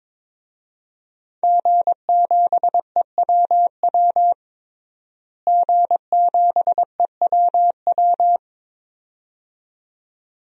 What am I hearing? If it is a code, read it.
Morse code, heard as a keyed tone, about 21 wpm, 700 Hz. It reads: G7EWW G7EWW
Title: DXCC-100 Call Signs - 5 Characters - Repeated 2x - 22wpm